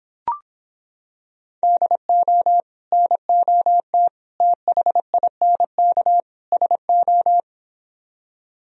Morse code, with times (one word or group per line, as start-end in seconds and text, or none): 1.6-2.6 DO
2.9-4.1 NOT
4.4-6.2 THINK
6.5-7.4 SO